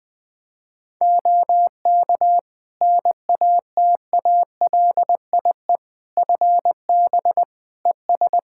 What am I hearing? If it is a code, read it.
Morse code, heard as a keyed tone, about 21 wpm, 700 Hz. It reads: OK NATALIE FB ES
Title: Practice QSO - Exchange 2 - Info - 20wpm